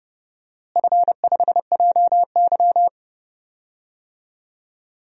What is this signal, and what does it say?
Morse code, heard as a keyed tone, about 30 wpm, 700 Hz. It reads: F5JY
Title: DXCC-100 Call Signs - 4 Characters - Repeated 2x - 30wpm